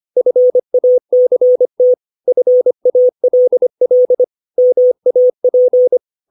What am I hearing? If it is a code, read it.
Morse code, heard as a keyed tone, about 25 wpm, 500 Hz. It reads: FACT FALL MAP
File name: 632 words / 632 Words 25WPM